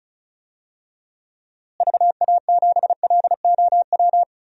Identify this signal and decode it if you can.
Morse code, heard as a keyed tone, about 36 wpm, 700 Hz. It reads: VA7LOW
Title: DXCC-100 Call Signs - 6 Characters - Repeated 3x - 35wpm